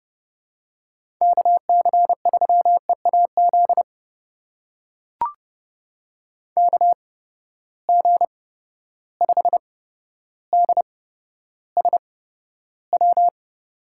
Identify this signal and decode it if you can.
Morse code, heard as a keyed tone, about 30 wpm, spaced as for 13 wpm, 700 Hz. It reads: KC3EAZ KG5DSW
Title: USA Call Signs - 6 Characters - 8x Character Spacing - 30wpm